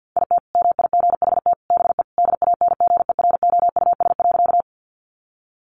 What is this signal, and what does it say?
Morse code, heard as a keyed tone, about 50 wpm, 700 Hz. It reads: IT MIGHT BE DANGEROUS.
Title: Sentences from Top 1200 Words 50wpm